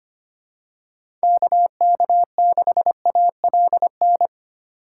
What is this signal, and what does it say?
Morse code, heard as a keyed tone, about 25 wpm, 700 Hz. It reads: KK6ALN
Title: USA Call Signs - 6 Characters - 6x Character Spacing - 25wpm